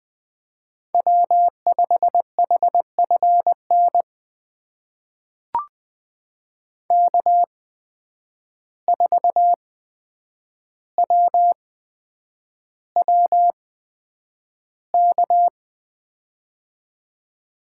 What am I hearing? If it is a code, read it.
Morse code, heard as a keyed tone, about 20 wpm, spaced as for 9 wpm, 700 Hz. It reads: W5HFN K4WWK
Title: USA Call Signs - 5 Characters - 8x Character Spacing - 20wpm